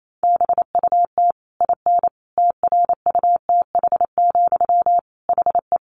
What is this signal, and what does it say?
Morse code, heard as a keyed tone, about 28 wpm, 700 Hz. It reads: BUT IN TRUTH, HE
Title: Sentences from Top 1500 Words 28wpm